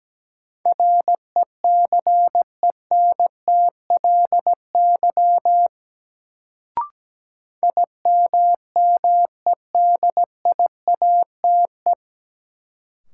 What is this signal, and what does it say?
Morse code, heard as a keyed tone, about 17 wpm, 700 Hz. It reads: RECENTLY IMMEDIATE
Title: Sentences from Top 1200 Words - New Word Drill - 17wpm